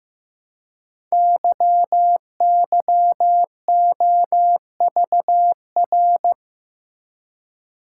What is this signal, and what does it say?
Morse code, heard as a keyed tone, about 15 wpm, 700 Hz. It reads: YYOVR
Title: Code Groups - 5 Letters - 15wpm